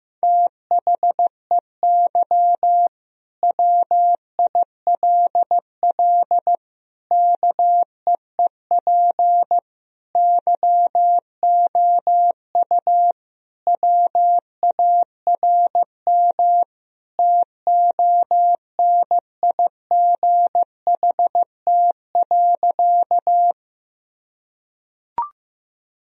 Morse code, heard as a keyed tone, about 16 wpm, 700 Hz. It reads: THEY WILL KEEP YOU WARM TONIGHT.
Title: Sentences from Top 1000 Words - Review Part 3 - 15wpm